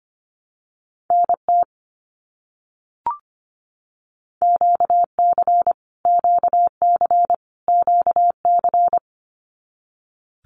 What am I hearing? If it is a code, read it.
Morse code, heard as a keyed tone, about 25 wpm, 700 Hz. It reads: NT QC QC QC
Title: Canadian Provinces and Territories - Repeated 3x - 25wpm